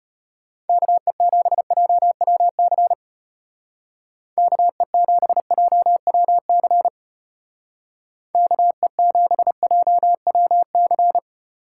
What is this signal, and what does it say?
Morse code, heard as a keyed tone, about 33 wpm, 700 Hz. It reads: KE7JWC KE7JWC KE7JWC